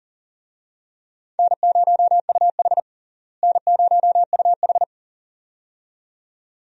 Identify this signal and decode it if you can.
Morse code, heard as a keyed tone, about 37 wpm, 700 Hz. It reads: N0UH N0UH
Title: USA Call Signs - 4 Characters - Repeated 2x - 40wpm